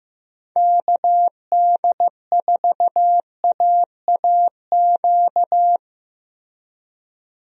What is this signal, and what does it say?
Morse code, heard as a keyed tone, about 16 wpm, 700 Hz. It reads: KD4AAQ
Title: USA Call Signs - 6 Characters - Repeated 3x - 15wpm